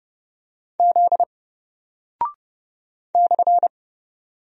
Morse code, heard as a keyed tone, about 30 wpm, 700 Hz. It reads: Z /
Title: Single Character (Letter:Number:Punctuation:Prosign) Alt 30wpm